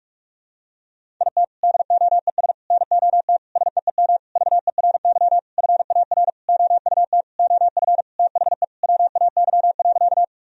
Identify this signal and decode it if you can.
Morse code, heard as a keyed tone, about 43 wpm, 700 Hz. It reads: IT DOES NOT SEEM VERY FAR OUT OF THE WAY.
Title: Sentences from Top 1000 Words - Review Part 1 - 45wpm